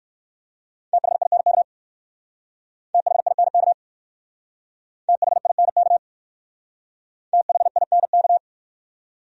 Morse code, heard as a keyed tone, about 59 wpm, 700 Hz. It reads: THINK THINK THINK THINK